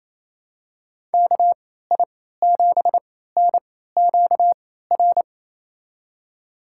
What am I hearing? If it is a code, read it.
Morse code, heard as a keyed tone, about 28 wpm, spaced as for 16 wpm, 700 Hz. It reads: KI7NQR
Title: USA Call Signs - 6 Characters - 3x Character Spacing - 28wpm